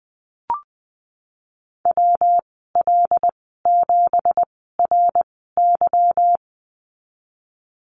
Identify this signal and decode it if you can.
Morse code, heard as a keyed tone, about 20 wpm, spaced as for 15 wpm, 700 Hz. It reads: WL7RY
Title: 186 Lesson 30g - Letter b - Call Signs 20wpm